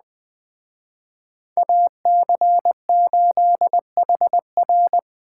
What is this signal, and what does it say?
Morse code, heard as a keyed tone, about 20 wpm, 700 Hz. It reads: AC8HR
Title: Speed Racing - USA 2x2 Call Signs 17wpm